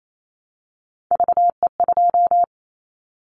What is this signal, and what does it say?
Morse code, heard as a keyed tone, about 28 wpm, 700 Hz. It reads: VE2
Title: Callsign Prefixes for Top 25 DXCC - 28wpm